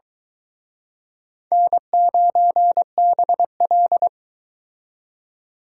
Morse code, heard as a keyed tone, about 24 wpm, 700 Hz. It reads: N9BL